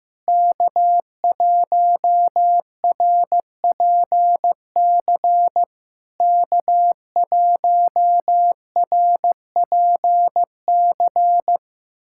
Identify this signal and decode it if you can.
Morse code, heard as a keyed tone, about 15 wpm, 700 Hz. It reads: K1RPC K1RPC
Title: SOTA QSO Practice 15wpm